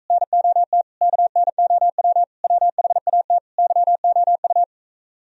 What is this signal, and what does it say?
Morse code, heard as a keyed tone, about 45 wpm, 700 Hz. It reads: NOT KNOW WHAT YOU